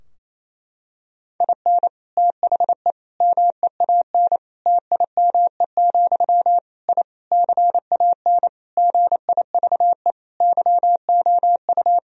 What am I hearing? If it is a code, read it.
Morse code, heard as a keyed tone, about 28 wpm, 700 Hz. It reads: IN THE MEAN TIME, I CAN GIVE YOU
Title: Sentences from Top 1000 Words - Review Part 3 - 28wpm